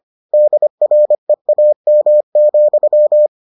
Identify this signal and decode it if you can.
Morse code, heard as a keyed tone, about 25 wpm, 600 Hz. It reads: DREAM,